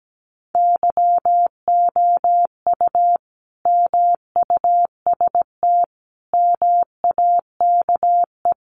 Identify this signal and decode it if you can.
Morse code, heard as a keyed tone, about 17 wpm, 700 Hz. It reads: YOU MUST MAKE
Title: Sentences from Top 1100 Words 17wpm